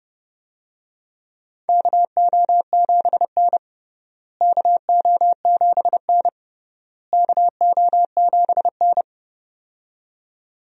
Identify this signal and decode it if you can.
Morse code, heard as a keyed tone, about 30 wpm, 700 Hz. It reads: KO7N KO7N KO7N
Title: USA Call Signs - 4 Characters - Repeated 3x - 30wpm